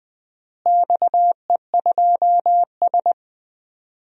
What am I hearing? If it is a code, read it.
Morse code, heard as a keyed tone, about 20 wpm, 700 Hz. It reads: XE2S
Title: DXCC-100 Call Signs - 4 Characters - Repeated 3x - 20wpm